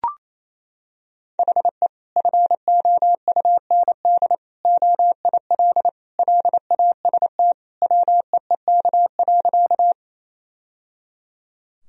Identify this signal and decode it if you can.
Morse code, heard as a keyed tone, about 28 wpm, 700 Hz. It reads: HE FOUND OIL LAST WEEK.
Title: Sentences from Top 1000 Words - Review Part 4 - 28wpm